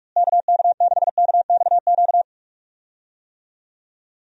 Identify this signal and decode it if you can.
Morse code, heard as a keyed tone, about 42 wpm, 700 Hz. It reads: KK6KXQ
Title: USA Call Signs - 6 Characters - Repeated 2x - 45wpm